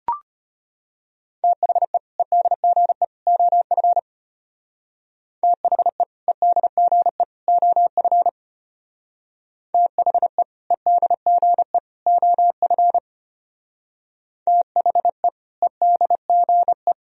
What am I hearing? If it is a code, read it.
Morse code, heard as a keyed tone, about 30 wpm, 700 Hz. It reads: THE EDGE OF THE EDGE OF THE EDGE OF THE EDGE
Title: Speed-Racing - Sets of 3 Words 25wpm